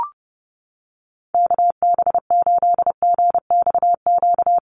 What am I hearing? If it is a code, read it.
Morse code, heard as a keyed tone, about 30 wpm, 700 Hz. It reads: KB8GXQ
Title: USA Call Signs Repeated 2x 30wpm